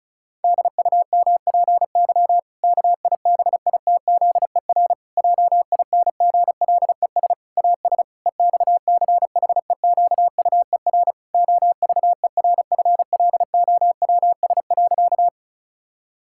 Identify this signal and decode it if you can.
Morse code, heard as a keyed tone, about 36 wpm, 700 Hz. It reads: DUMPY KIBITZER JINGLES AS EXCHEQUER OVERFLOWS.
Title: Pangram Challenge v2 - 35wpm